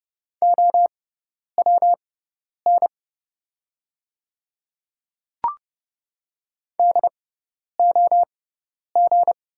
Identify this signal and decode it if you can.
Morse code, heard as a keyed tone, about 30 wpm, spaced as for 10 wpm, 700 Hz. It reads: OWN DOG
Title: 6x Character Spacing - Top 500 Words - 30wpm